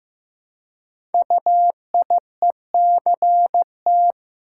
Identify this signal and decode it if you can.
Morse code, heard as a keyed tone, about 15 wpm, 700 Hz. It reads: UIECT
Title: Code Groups - 5 Letters - 15wpm